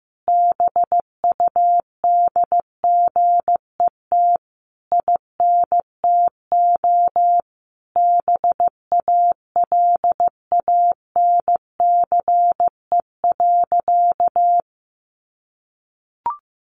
Morse code, heard as a keyed tone, about 15 wpm, 700 Hz. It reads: BUDGET INTO BALANCE.
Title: Sentences from Top 1200 Words 15wpm